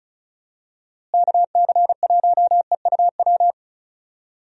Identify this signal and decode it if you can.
Morse code, heard as a keyed tone, about 34 wpm, 700 Hz. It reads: KC1EUW
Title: USA Call Signs - 6 Characters - Repeated 3x - 35wpm